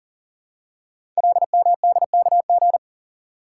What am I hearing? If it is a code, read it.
Morse code, heard as a keyed tone, about 41 wpm, 700 Hz. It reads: LMDKG
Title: Code Groups - 5 Letters - 40wpm